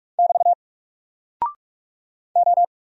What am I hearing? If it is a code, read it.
Morse code, heard as a keyed tone, about 47 wpm, 700 Hz. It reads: <BT> O
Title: Single Character (Letter:Number:Punctuation:Prosign) 45wpm